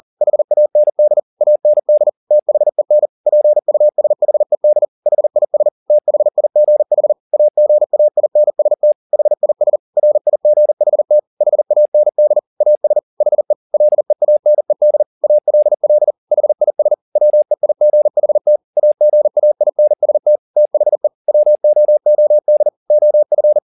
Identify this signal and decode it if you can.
Morse code, heard as a keyed tone, about 40 wpm, 600 Hz. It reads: HAND AND THEN PUSHED HIS THIGH AGAINST HIS RIGHT HAND AS HE LEANED ALL HIS WEIGHT AGAINST THE WOOD OF
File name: the old man and the sea 2 40wpm